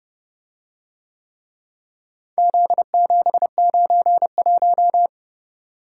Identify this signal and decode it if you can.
Morse code, heard as a keyed tone, about 29 wpm, 700 Hz. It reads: Z791